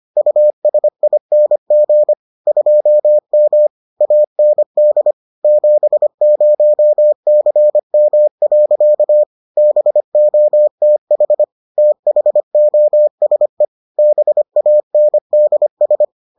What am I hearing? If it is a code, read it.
Morse code, heard as a keyed tone, about 25 wpm, 600 Hz. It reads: USING 2M AND 70CM. BOTH THOSE BANDS